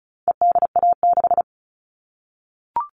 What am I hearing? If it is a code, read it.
Morse code, heard as a keyed tone, about 34 wpm, 700 Hz. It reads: EDA6
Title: All Cut Numbers - 4-Digit 35wpm